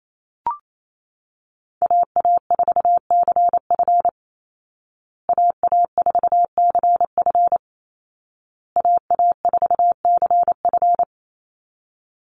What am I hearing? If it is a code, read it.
Morse code, heard as a keyed tone, about 28 wpm, 700 Hz. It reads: AA4CF AA4CF AA4CF
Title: Call Signs Repeated 2x - 28wpm